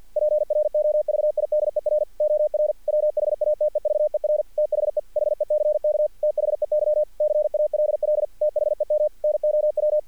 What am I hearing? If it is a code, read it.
Morse code, heard as a keyed tone, about 46 wpm, 600 Hz. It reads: PROVIDER OR WHATEVER THE HECK THEY CALL THEM NOW
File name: archived / CRSnet